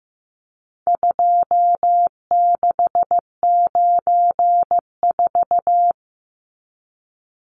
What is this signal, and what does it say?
Morse code, heard as a keyed tone, about 15 wpm, 700 Hz. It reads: 2694
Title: Numbers - 4-Digit 15wpm